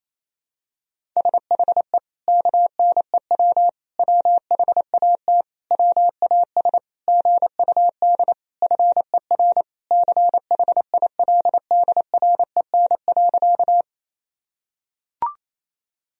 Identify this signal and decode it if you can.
Morse code, heard as a keyed tone, about 28 wpm, 700 Hz. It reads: SHE KNEW WHAT WAS GUD FER CHILDREN.
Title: Sentences from Top 400 Words and Abbreviations 28wpm